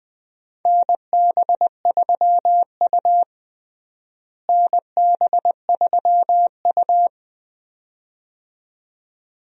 Morse code, heard as a keyed tone, about 20 wpm, 700 Hz. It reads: NB3U NB3U
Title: USA Call Signs - 4 Characters - Repeated 2x - 20wpm